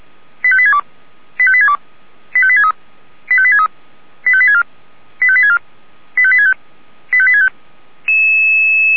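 Five-tone selective call (ZVEI), heard as quick runs of five tones, tones of about 70 ms. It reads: 85862 85862 85863 85863 85864 85864 85865 85865
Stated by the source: ZVEI five-tone selective call received by SDR.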